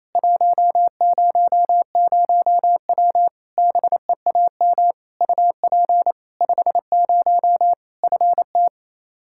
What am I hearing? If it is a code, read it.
Morse code, heard as a keyed tone, about 28 wpm, 700 Hz. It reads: 100W BEAM UP 50 FT
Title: Practice QSO - Exchange 2 - Info - 28wpm